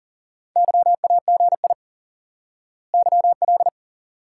Morse code, heard as a keyed tone, about 40 wpm, 700 Hz. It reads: YAGI YL
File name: IQE Course 3 - Lesson 89 - YAGI - Two Elements - 40wpm